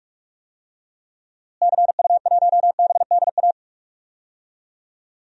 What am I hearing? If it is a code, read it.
Morse code, heard as a keyed tone, about 46 wpm, 700 Hz. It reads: CU1BDA
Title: DXCC-100 Call Signs - 6 Characters - Repeated 2x - 45wpm